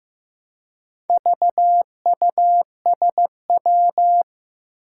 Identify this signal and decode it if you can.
Morse code, heard as a keyed tone, about 15 wpm, 700 Hz. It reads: VUSW
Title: Four Challenging Letter-Number Combinations 15wpm